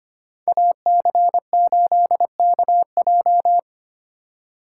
Morse code, heard as a keyed tone, about 25 wpm, 700 Hz. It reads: AC8KJ
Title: USA Call Signs - 2x Character Spacing - 25wpm